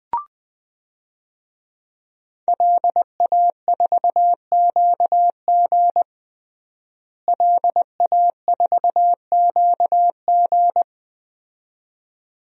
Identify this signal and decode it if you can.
Morse code, heard as a keyed tone, about 21 wpm, 700 Hz. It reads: LA4QG LA4QG
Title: DXCC-100 Call Signs - 5 Characters - Repeated 2x - 20wpm